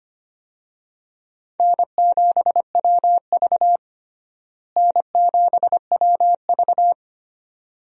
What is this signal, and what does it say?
Morse code, heard as a keyed tone, about 25 wpm, 700 Hz. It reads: N7WV N7WV
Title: USA Call Signs - 4 Characters - Repeated 2x - 25wpm